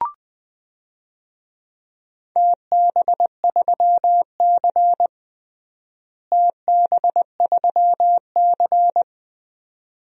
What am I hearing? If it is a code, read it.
Morse code, heard as a keyed tone, about 20 wpm, 700 Hz. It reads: TB3C TB3C
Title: DXCC-100 Call Signs - 4 Characters - Repeated 2x - 20wpm